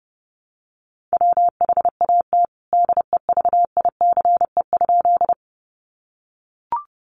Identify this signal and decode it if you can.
Morse code, heard as a keyed tone, about 30 wpm, 700 Hz. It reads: WHAT DEVICE?
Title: Sentences from Top 1500 Words 30wpm